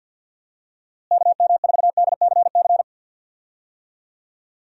Morse code, heard as a keyed tone, about 49 wpm, 700 Hz. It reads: KM4DCC
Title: USA Call Signs - 6 Characters - Repeated 3x - 50wpm